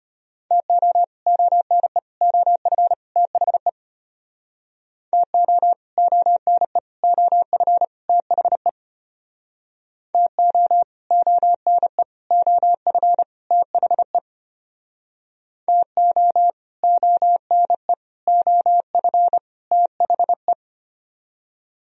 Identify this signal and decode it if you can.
Morse code, heard as a keyed tone, about 32 wpm, 700 Hz. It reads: TO ONE OF THE TO ONE OF THE TO ONE OF THE TO ONE OF THE